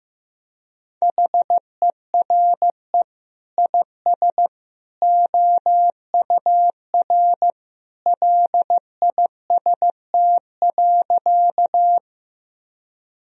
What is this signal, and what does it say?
Morse code, heard as a keyed tone, about 15 wpm, 700 Hz. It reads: HERE IS OUR LIST.
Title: Sentences from Top 1000 Words - Review Part 2 - 15wpm